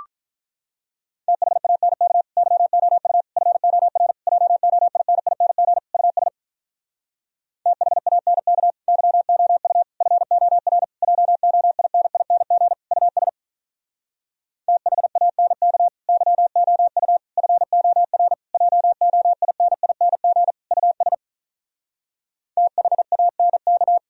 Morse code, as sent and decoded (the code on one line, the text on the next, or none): - .... .- -. -.- / -.-- --- ..- / ..-. --- .-. / .--- --- .. -. .. -. --. / ..- ... / - .... .- -. -.- / -.-- --- ..- / ..-. --- .-. / .--- --- .. -. .. -. --. / ..- ... / - .... .- -. -.- / -.-- --- ..- / ..-. --- .-. / .--- --- .. -. .. -. --. / ..- ... / - .... .- -. -.-
THANK YOU FOR JOINING US THANK YOU FOR JOINING US THANK YOU FOR JOINING US THANK